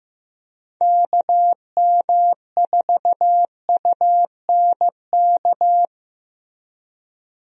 Morse code, heard as a keyed tone, about 16 wpm, 700 Hz. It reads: KM4UNK
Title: USA Call Signs - 6 Characters - Repeated 2x - 15wpm